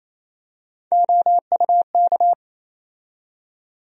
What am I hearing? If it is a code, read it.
Morse code, heard as a keyed tone, about 28 wpm, 700 Hz. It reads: OUK
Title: USA Call Sign Postfixes - 3 Letters 28wpm